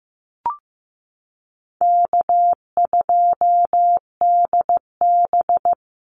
Morse code, heard as a keyed tone, about 15 wpm, 700 Hz. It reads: K2DB
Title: Easy Call Signs Repeated 3x - 15wpm